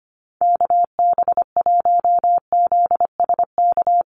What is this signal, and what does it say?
Morse code, heard as a keyed tone, about 25 wpm, 700 Hz. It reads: KB1ZSK
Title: USA Call Signs Repeated 2x 25wpm